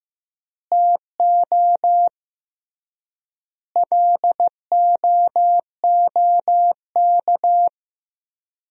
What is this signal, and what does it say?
Morse code, heard as a keyed tone, about 15 wpm, 700 Hz. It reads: TO LOOK
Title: 3x Word-Spacing - Sentences from Top 200 Words - 15wpm